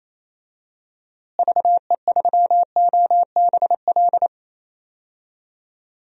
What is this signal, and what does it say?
Morse code, heard as a keyed tone, about 28 wpm, 700 Hz. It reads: VE3OBL
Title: DXCC-100 Call Signs - 6 Characters - Repeated 2x - 28wpm